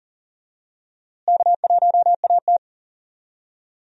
Morse code, heard as a keyed tone, about 40 wpm, 700 Hz. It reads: K1AT
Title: USA Call Signs - 4 Characters - Repeated 3x - 40wpm